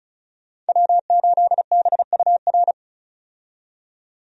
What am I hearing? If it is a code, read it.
Morse code, heard as a keyed tone, about 36 wpm, 700 Hz. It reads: W8BUR
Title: USA Call Signs - 5 Characters - 35wpm